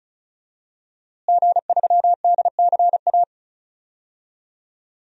Morse code, heard as a keyed tone, about 36 wpm, 700 Hz. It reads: G3DCA
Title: DXCC-100 Call Signs - 5 Characters - Repeated 3x - 35wpm